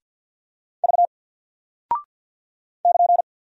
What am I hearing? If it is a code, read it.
Morse code, heard as a keyed tone, about 48 wpm, 700 Hz. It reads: V <KN>
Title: Single Character (Letter:Number:Punctuation:Prosign) Alt 50wpm